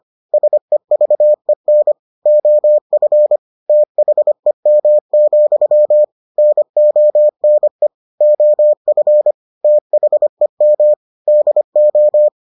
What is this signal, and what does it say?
Morse code, heard as a keyed tone, about 25 wpm, 600 Hz. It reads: SEVEN OF THEM, NONE OF THEM DO